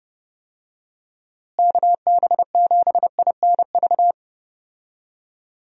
Morse code, heard as a keyed tone, about 31 wpm, 700 Hz. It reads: KB7INV
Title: USA Call Signs - 6 Characters - Repeated 2x - 30wpm